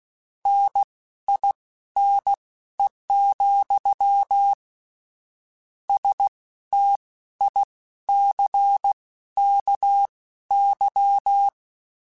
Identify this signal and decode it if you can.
Morse code, heard as a keyed tone, about 16 wpm, spaced as for 12 wpm, 800 Hz. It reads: NINE, STICKY
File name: Misc / Misc-mp3s / WordPairs sp @15